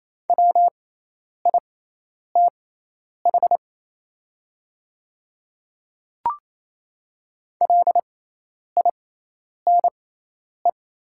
Morse code, heard as a keyed tone, about 28 wpm, spaced as for 10 wpm, 700 Hz. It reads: WITH LINE
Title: 6x Character Spacing - Top 500 Words - 28wpm